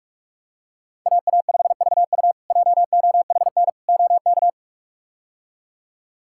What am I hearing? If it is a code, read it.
Morse code, heard as a keyed tone, about 46 wpm, 700 Hz. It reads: AA5VU JOHN OK
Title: K1USN SST QSO Practice - 45wpm